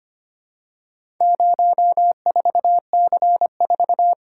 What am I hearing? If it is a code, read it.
Morse code, heard as a keyed tone, about 25 wpm, 700 Hz. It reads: 04C4
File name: practice / 25wpm / Four Letter-Number Combinations - 25wpm